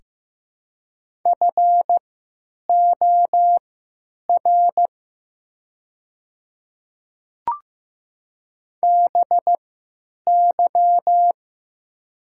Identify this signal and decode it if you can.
Morse code, heard as a keyed tone, about 15 wpm, spaced as for 8 wpm, 700 Hz. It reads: FOR BY
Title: 3x Character Spacing - Top 100 Words - 15wpm